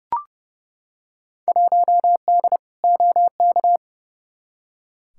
Morse code, heard as a keyed tone, about 30 wpm, 700 Hz. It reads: 1D OK
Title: Field Day Exchanges - 30wpm